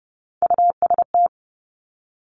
Morse code, heard as a keyed tone, about 30 wpm, 700 Hz. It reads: UST
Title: Three Challenging Letter-Number Combinations 30wpm